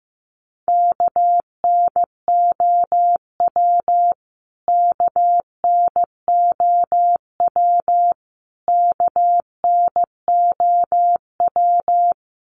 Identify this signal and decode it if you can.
Morse code, heard as a keyed tone, about 15 wpm, 700 Hz. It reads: KNOW KNOW KNOW
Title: Top 300 Words Repeated 15wpm